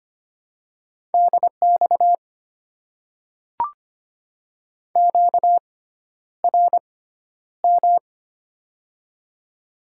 Morse code, heard as a keyed tone, about 25 wpm, spaced as for 11 wpm, 700 Hz. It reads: DX QRM
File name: Top 90 QSO Elements - 6x Character Spacing - 25wpm